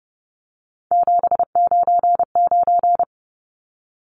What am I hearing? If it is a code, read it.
Morse code, heard as a keyed tone, about 30 wpm, 700 Hz. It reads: 799
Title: Numbers - 3-Digit 30wpm